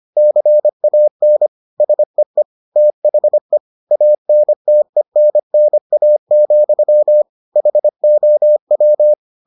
Morse code, heard as a keyed tone, about 25 wpm, 600 Hz. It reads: CAN SEE THE ANTENNA, HOW